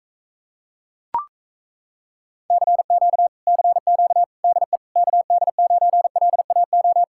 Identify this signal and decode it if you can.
Morse code, heard as a keyed tone, about 37 wpm, 700 Hz. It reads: CQ CQ DE KD9LAO